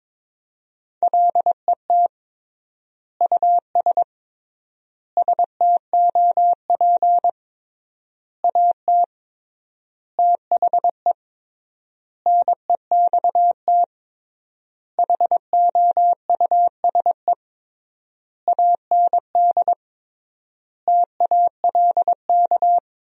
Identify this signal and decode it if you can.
Morse code, heard as a keyed tone, about 22 wpm, 700 Hz. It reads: LET US STOP AT THE NEXT HOUSE AND TALK
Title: 3x Word-Spacing - Sentences from Top 400 Words - 22wpm